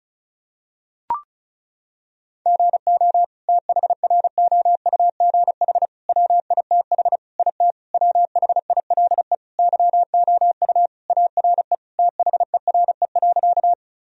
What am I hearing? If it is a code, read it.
Morse code, heard as a keyed tone, about 34 wpm, 700 Hz. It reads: GO THROUGH WITH IT WHILE YOU ARE THERE.
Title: Sentences from Top 1000 Words - Review Part 1 - 35wpm